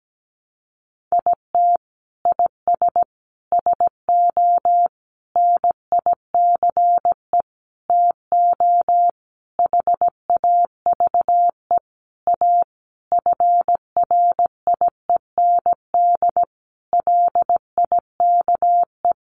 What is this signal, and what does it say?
Morse code, heard as a keyed tone, about 17 wpm, 700 Hz. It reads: IT IS SO NICE TO HAVE A FRIEND LIKE
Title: Sentences from Top 1100 Words 17wpm